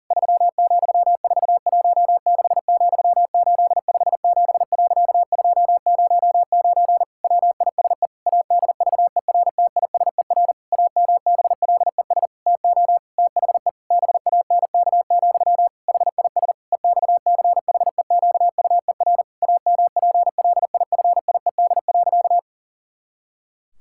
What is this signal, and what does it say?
Morse code, heard as a keyed tone, about 40 wpm, 700 Hz. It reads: 3,416,857.209 WISE ADVERTISER AMBLES TO THE BANK, HIS EXCHEQUER AMPLIFIED.
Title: Pangram Challenge v2 - 40wpm